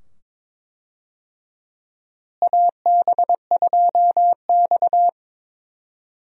Morse code, heard as a keyed tone, about 22 wpm, 700 Hz. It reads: AB2X
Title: USA Call Signs - 4 Characters - Repeated 2x - 22wpm